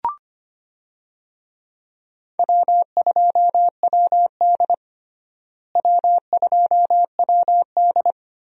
Morse code, heard as a keyed tone, about 25 wpm, 700 Hz. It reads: W2WD W2WD
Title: USA Call Signs - 4 Characters - Repeated 2x - 25wpm